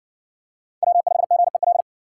Morse code, heard as a keyed tone, about 52 wpm, 700 Hz. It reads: W5GEL